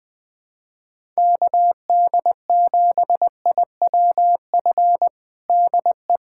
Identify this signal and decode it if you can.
Morse code, heard as a keyed tone, about 20 wpm, 700 Hz. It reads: KD7IWF DE
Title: USA Callsign DE Callsign - 6 Characters - 20wpm